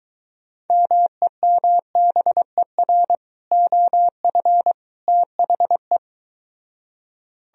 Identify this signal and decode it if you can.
Morse code, heard as a keyed tone, about 23 wpm, 700 Hz. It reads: MEMBER OF THE